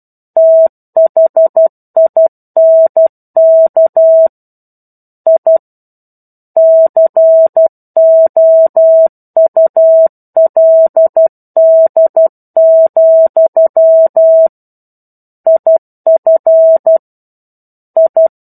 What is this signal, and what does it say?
Morse code, heard as a keyed tone, about 12 wpm, 650 Hz. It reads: THINK I COULD, IF I
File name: 01-Chapter-01 12 WPM